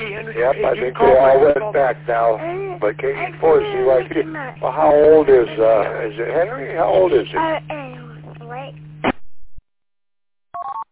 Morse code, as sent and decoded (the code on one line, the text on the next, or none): none